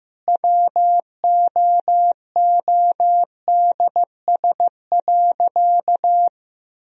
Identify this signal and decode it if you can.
Morse code, heard as a keyed tone, about 16 wpm, 700 Hz. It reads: WOODS.
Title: Sentences from Top 1000 Words - Review Part 2 - 15wpm